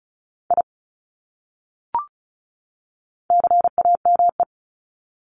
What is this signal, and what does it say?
Morse code, heard as a keyed tone, about 34 wpm, 700 Hz. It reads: I CAME
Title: Top 300 Words 35wpm